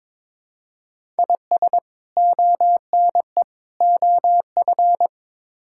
Morse code, heard as a keyed tone, about 22 wpm, 700 Hz. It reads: IS ONE OF
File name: practice / 22wpm / Build A Phrase - 5-Word Phrases - 22wpm